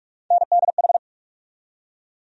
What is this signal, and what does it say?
Morse code, heard as a keyed tone, about 46 wpm, 700 Hz. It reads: NDH
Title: USA Call Sign Postfixes - 3 Letters 45wpm